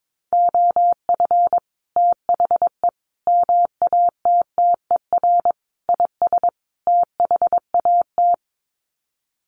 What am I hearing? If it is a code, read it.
Morse code, heard as a keyed tone, about 22 wpm, 700 Hz. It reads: OF THE MATTER IS THAT
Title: Sets of 5 Words Encore 22wpm